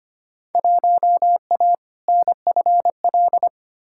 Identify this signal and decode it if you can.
Morse code, heard as a keyed tone, about 25 wpm, 700 Hz. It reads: 1A NFL
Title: Field Day Exchanges - 25wpm